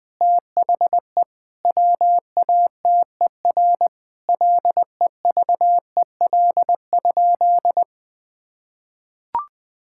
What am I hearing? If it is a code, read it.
Morse code, heard as a keyed tone, about 20 wpm, 700 Hz. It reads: THE WATER LEVEL?
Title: Sentences from Top 1000 Words - Review Part 5 - 20wpm